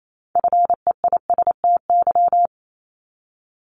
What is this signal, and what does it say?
Morse code, heard as a keyed tone, about 28 wpm, 700 Hz. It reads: FEISTY
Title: Multi-letter Phonemes 28wpm